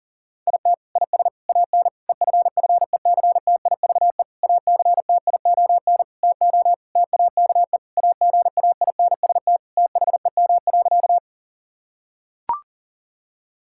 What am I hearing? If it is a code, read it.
Morse code, heard as a keyed tone, about 40 wpm, 700 Hz. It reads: IT IS AN EFFECTIVE ACTION TO TAKE AGAINST THEM.
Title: Sentences from Top 1000 Words - Review Part 6 - 40wpm